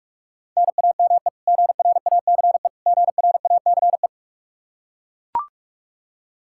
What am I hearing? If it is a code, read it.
Morse code, heard as a keyed tone, about 41 wpm, 700 Hz. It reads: NAME GRACE GRACE
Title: Practice QSO - Exchange 1 - Beginning - 45wpm